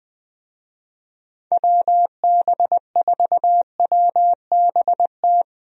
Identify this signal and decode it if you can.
Morse code, heard as a keyed tone, about 20 wpm, 700 Hz. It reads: WB4WBT
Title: USA Call Signs - 6 Characters - Repeated 3x - 20wpm